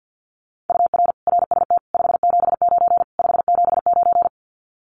Morse code, heard as a keyed tone, about 50 wpm, 700 Hz. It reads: UR RST 579 579
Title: Common QSO Phrases Encore 50wpm